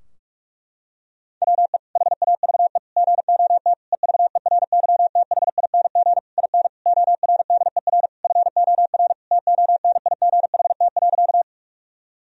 Morse code, heard as a keyed tone, about 41 wpm, 700 Hz. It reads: WE HAVE GOT EVERYTHING IN ORDER FOR TONIGHT.
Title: Sentences from Top 1000 Words - Review Part 3 - 45wpm